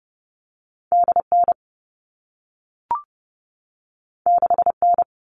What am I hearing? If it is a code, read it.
Morse code, heard as a keyed tone, about 30 wpm, 700 Hz. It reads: DN 6N
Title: All Cut Numbers - 2-Digit 30wpm